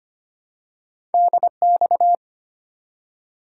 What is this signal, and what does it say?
Morse code, heard as a keyed tone, about 25 wpm, 700 Hz. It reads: DX
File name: Top 90 QSO Elements - 3x Character Spacing - 25wpm